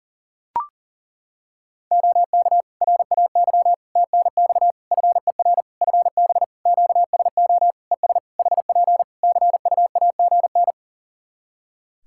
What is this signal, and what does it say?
Morse code, heard as a keyed tone, about 40 wpm, 700 Hz. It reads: OK RAY TNX FER FB QSO ES HP CUAGN
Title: Practice QSO - Exchange 4 - Ending - 40wpm